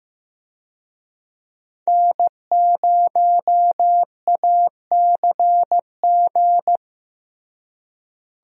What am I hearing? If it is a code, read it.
Morse code, heard as a keyed tone, about 15 wpm, 700 Hz. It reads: N0ACG
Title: USA Call Signs - 5 Characters - Repeated 3x - 15wpm